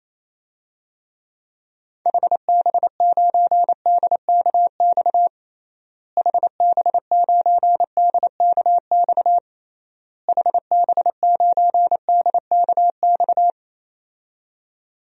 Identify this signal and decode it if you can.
Morse code, heard as a keyed tone, about 27 wpm, 700 Hz. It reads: HB9DKX HB9DKX HB9DKX
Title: DXCC-100 Call Signs - 6 Characters - Repeated 3x - 28wpm